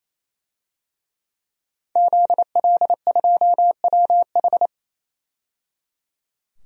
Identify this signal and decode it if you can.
Morse code, heard as a keyed tone, about 28 wpm, 700 Hz. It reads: ZL2WH
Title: DXCC-100 Call Signs - 5 Characters - 28wpm